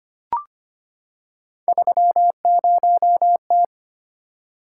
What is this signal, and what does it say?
Morse code, heard as a keyed tone, about 25 wpm, 700 Hz. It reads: 30T
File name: Three Letter-Number Combinations - 25wpm